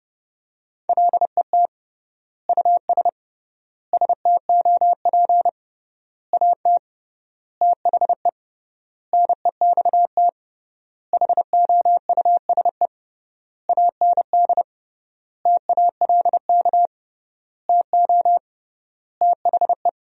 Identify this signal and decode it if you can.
Morse code, heard as a keyed tone, about 30 wpm, 700 Hz. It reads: LET US STOP AT THE NEXT HOUSE AND TALK TO THE
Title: 3x Word-Spacing - Sentences from Top 400 Words - 30wpm